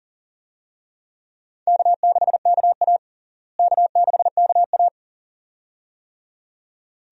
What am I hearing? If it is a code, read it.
Morse code, heard as a keyed tone, about 37 wpm, 700 Hz. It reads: K6KA K6KA
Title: USA Call Signs - 4 Characters - Repeated 2x - 40wpm